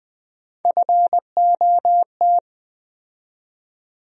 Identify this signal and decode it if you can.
Morse code, heard as a keyed tone, about 20 wpm, 700 Hz. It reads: FOT
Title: USA Call Sign Postfixes - 3 Letters 20wpm